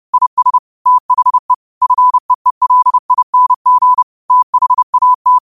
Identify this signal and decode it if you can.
Morse code, heard as a keyed tone, about 30 wpm, 1000 Hz. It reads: IS THE FEELING THAT
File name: qotdfiles / 30wpm